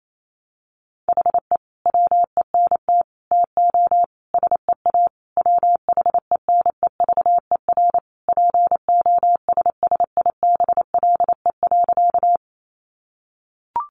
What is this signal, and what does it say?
Morse code, heard as a keyed tone, about 28 wpm, 700 Hz. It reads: HE WENT TO SEA WHENEVER POSSIBLE.
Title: Sentences from Top 1200 Words 28wpm